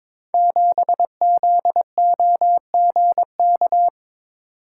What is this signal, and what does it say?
Morse code, heard as a keyed tone, about 22 wpm, 700 Hz. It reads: 7ZOGK
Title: Code Groups - 5 Letter:Numbers - 22wpm